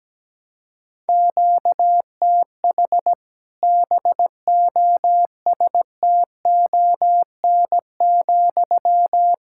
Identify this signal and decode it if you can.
Morse code, heard as a keyed tone, about 17 wpm, 700 Hz. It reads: QTH BOSTON,
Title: Practice QSO - Exchange 1 - Beginning - 17wpm